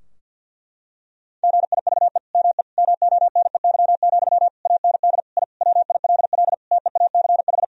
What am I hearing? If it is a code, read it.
Morse code, heard as a keyed tone, about 48 wpm, 700 Hz. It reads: GIVE ME MONEY, AND I WILL TEACH
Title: Sentences from Top 1000 Words - Review Part 2 - 50wpm